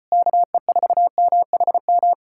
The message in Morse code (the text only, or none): KE4MHM